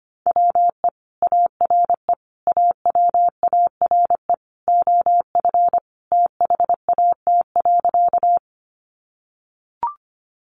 Morse code, heard as a keyed tone, about 25 wpm, 700 Hz. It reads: WE ARE AWARE OF THAT.
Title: Sentences from Top 1200 Words 25wpm